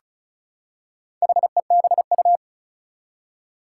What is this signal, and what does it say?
Morse code, heard as a keyed tone, about 36 wpm, 700 Hz. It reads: HEBU
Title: Four Challenging Letter-Number Combinations 35wpm